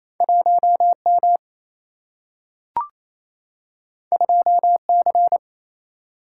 Morse code, heard as a keyed tone, about 28 wpm, 700 Hz. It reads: 1 M 2 C
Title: Single-Letter Single-Number 28wpm